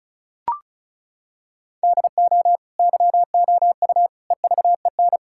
Morse code, heard as a keyed tone, about 34 wpm, 700 Hz. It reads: DO YOU EVEN
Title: Sentences from Top 1000 Words - Review Part 5 - 35wpm